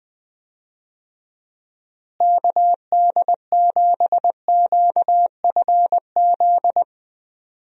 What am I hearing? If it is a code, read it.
Morse code, heard as a keyed tone, about 20 wpm, 700 Hz. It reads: KD7QFZ
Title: USA Call Signs - 6 Characters - Repeated 3x - 20wpm